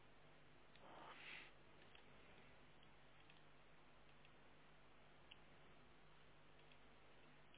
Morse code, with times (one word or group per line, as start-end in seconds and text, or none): none